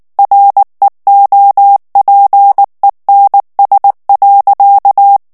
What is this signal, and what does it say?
Morse code, heard as a keyed tone, about 19 wpm, 800 Hz. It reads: REOPENS.
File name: ANGLAIS texte clair 3 vitesse 18 mots minute